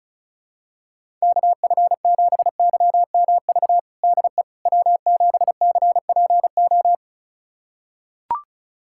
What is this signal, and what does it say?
Morse code, heard as a keyed tone, about 34 wpm, 700 Hz. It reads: KF7YMV DE W7CPO
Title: USA Callsign DE Callsign 35wpm